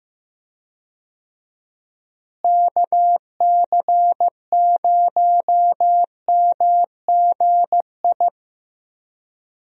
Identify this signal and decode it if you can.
Morse code, heard as a keyed tone, about 15 wpm, 700 Hz. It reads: KC0MGI
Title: USA Call Signs - 6 Characters - Repeated 3x - 15wpm